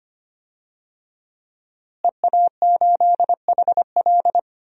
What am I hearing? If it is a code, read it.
Morse code, heard as a keyed tone, about 25 wpm, 700 Hz. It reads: EA8HL
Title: DXCC-100 Call Signs - 5 Characters - Repeated 2x - 25wpm